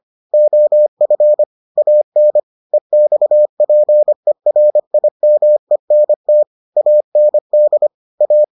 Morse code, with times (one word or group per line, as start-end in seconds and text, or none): 0.3-1.4 OF
1.8-2.4 AN
2.7-6.4 EXPERIMENT
6.8-7.9 AND
8.2-8.4 A